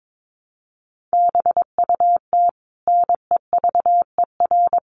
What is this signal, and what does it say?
Morse code, heard as a keyed tone, about 22 wpm, 700 Hz. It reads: BUT NEVER
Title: Sentences from Top 1100 Words 22wpm